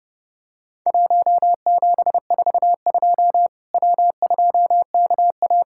Morse code, heard as a keyed tone, about 30 wpm, 700 Hz. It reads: 1742 W2KA
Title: Worked All Europe DX Contest - QTC Exchange - 30wpm